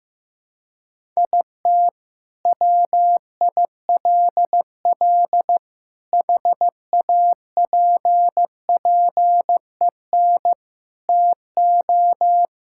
Morse code, heard as a keyed tone, about 15 wpm, 700 Hz. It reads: IT WILL HAPPEN TO
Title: Sentences from Top 1000 Words - Review Part 6 - 15wpm